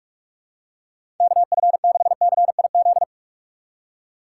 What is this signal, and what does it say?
Morse code, heard as a keyed tone, about 46 wpm, 700 Hz. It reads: KF6CIZ
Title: USA Call Signs - 6 Characters - 4x Character Spacing - 45wpm